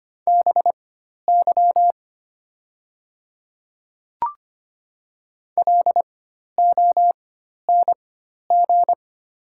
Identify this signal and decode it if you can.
Morse code, heard as a keyed tone, about 25 wpm, spaced as for 12 wpm, 700 Hz. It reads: BY LONG
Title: 4x Character Spacing - Top 200 Words - 25wpm